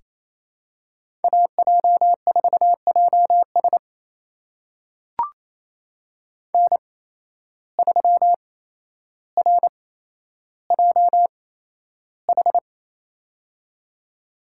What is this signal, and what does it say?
Morse code, heard as a keyed tone, about 28 wpm, spaced as for 12 wpm, 700 Hz. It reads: AJ4JS N3RJH
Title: USA Call Signs - 5 Characters - 8x Character Spacing - 28wpm